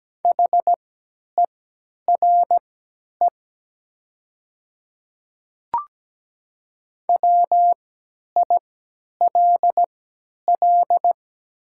Morse code, heard as a keyed tone, about 17 wpm, spaced as for 10 wpm, 700 Hz. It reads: HERE WILL
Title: 3x Character Spacing - Top 500 Words - 17wpm